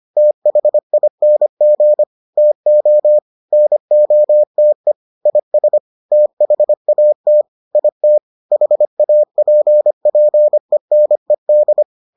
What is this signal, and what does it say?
Morse code, heard as a keyed tone, about 25 wpm, 600 Hz. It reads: THING TO NOTE IS THAT IT HAPPENED